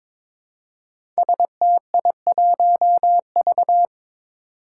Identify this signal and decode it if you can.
Morse code, heard as a keyed tone, about 23 wpm, 700 Hz. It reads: STI1V
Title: Five Challenging Letter-Number Combinations 22wpm